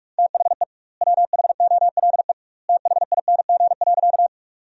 Morse code, heard as a keyed tone, about 48 wpm, 700 Hz. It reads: THE WHOLE THING.
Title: 2x Word-Spacing - Sentences from Top 400 Words - 45wpm